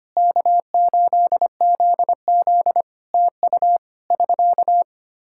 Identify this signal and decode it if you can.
Morse code, heard as a keyed tone, about 25 wpm, 700 Hz. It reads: K8ZZ TU <SK>
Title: Practice QSO - Exchange 4 - Ending - 25wpm